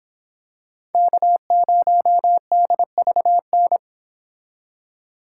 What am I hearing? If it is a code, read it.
Morse code, heard as a keyed tone, about 25 wpm, 700 Hz. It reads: K0DVN